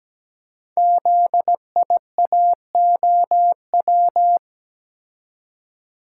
Morse code, heard as a keyed tone, about 17 wpm, 700 Hz. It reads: ZIAOW
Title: Code Groups - 5 Letter:Numbers - 17wpm